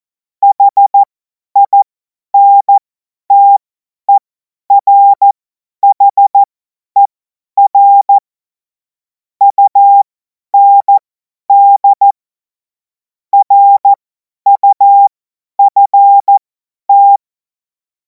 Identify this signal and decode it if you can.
Morse code, heard as a keyed tone, about 14 wpm, spaced as for 10 wpm, 800 Hz. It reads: HINTERHER UND RUFT
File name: Morsekurs / Witze / Witz 19. Mai 2026 50 F70